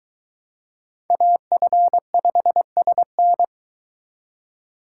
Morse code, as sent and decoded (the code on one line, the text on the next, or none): .- ..-. ..... ... -.
AF5SN